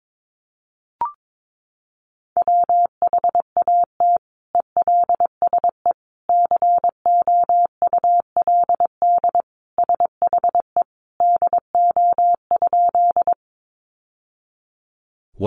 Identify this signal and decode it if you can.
Morse code, heard as a keyed tone, about 22 wpm, 700 Hz. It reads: WHAT ELSE COULD SHE DO?
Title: Sentences from Top 800 Words 22wpm